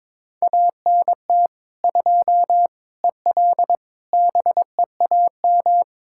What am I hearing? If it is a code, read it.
Morse code, heard as a keyed tone, about 22 wpm, 700 Hz. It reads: ANT 2 EL BEAM
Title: Practice QSO - Exchange 2 - Info - 22wpm